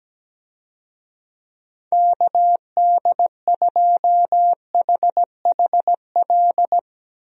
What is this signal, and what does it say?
Morse code, heard as a keyed tone, about 17 wpm, 700 Hz. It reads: KD2HHL
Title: USA Call Signs - 6 Characters - Repeated 3x - 17wpm